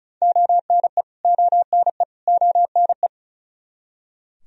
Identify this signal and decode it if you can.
Morse code, heard as a keyed tone, about 34 wpm, 700 Hz. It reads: ONE ONE ONE
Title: Top 100 Words Repeated 35wpm